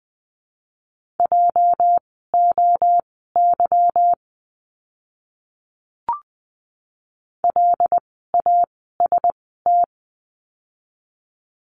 Morse code, heard as a keyed tone, about 20 wpm, spaced as for 15 wpm, 700 Hz. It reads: JOY LAST
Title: 206 Lesson 33f - Letter j - Words 20wpm